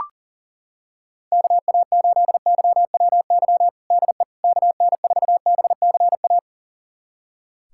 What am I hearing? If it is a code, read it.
Morse code, heard as a keyed tone, about 40 wpm, 700 Hz. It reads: KA8YWY DE KN4BCA
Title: USA Callsign DE Callsign - 6 Characters - 40wpm